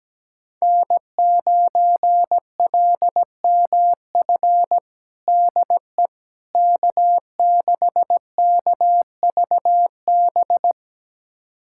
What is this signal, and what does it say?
Morse code, heard as a keyed tone, about 17 wpm, 700 Hz. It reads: N9LMF DE K6KVB
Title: USA Callsign DE Callsign - 5 Characters - 17wpm